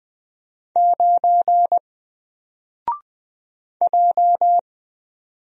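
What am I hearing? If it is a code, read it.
Morse code, heard as a keyed tone, about 20 wpm, 700 Hz. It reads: 9 J
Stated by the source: Single letter and number Morse drill at 20 wpm, from N4LCD.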